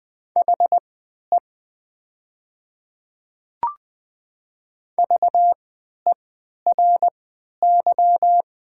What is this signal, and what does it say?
Morse code, heard as a keyed tone, about 20 wpm, spaced as for 11 wpm, 700 Hz. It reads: HE VERY
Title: 3x Character Spacing - Top 200 Words - 20wpm